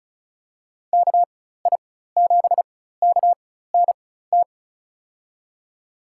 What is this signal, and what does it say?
Morse code, heard as a keyed tone, about 36 wpm, spaced as for 16 wpm, 700 Hz. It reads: KI7KNT
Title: USA Call Signs - 6 Characters - 4x Character Spacing - 35wpm